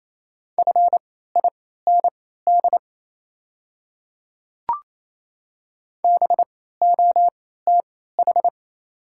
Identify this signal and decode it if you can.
Morse code, heard as a keyed tone, about 28 wpm, spaced as for 16 wpm, 700 Hz. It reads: FIND BOTH
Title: 3x Character Spacing - Top 500 Words - 28wpm